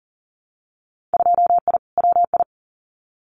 Morse code, heard as a keyed tone, about 41 wpm, 700 Hz. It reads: 2I WI
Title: Winter Field Day Exchanges 40wpm